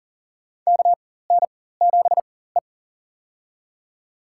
Morse code, heard as a keyed tone, about 40 wpm, spaced as for 19 wpm, 700 Hz. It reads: KN7E
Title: USA Call Signs - 4 Characters - 4x Character Spacing - 40wpm